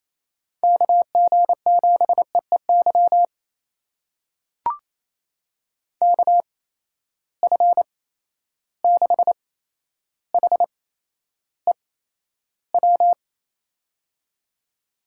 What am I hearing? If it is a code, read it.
Morse code, heard as a keyed tone, about 28 wpm, spaced as for 12 wpm, 700 Hz. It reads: KG7EEY KF6HEW
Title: USA Call Signs - 8x Character Spacing - 28wpm